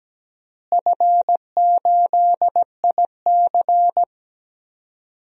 Morse code, heard as a keyed tone, about 17 wpm, 700 Hz. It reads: F8IC
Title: DXCC-100 Call Signs - 4 Characters - Repeated 2x - 17wpm